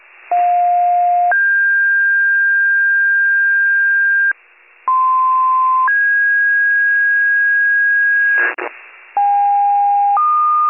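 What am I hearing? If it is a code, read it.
Two-tone page, heard as a one-second tone, then a three-second tone, about 710 then 1600 Hz, about 1040 then 1650 Hz. It is heard from 0.3 to 8.5 s.
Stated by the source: Rescue duty pager tones from a county-wide paging archive.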